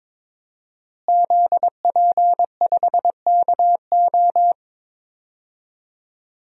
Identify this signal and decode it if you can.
Morse code, heard as a keyed tone, about 22 wpm, 700 Hz. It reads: ZP5KO
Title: DXCC-100 Call Signs - 5 Characters - Repeated 3x - 22wpm